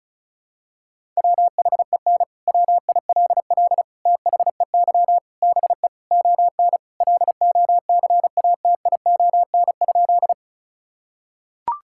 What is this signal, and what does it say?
Morse code, heard as a keyed tone, about 36 wpm, 700 Hz. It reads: WHEN WILL THEY BE ON LOCATION?
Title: Sentences from Top 1000 Words - Review Part 6 - 35wpm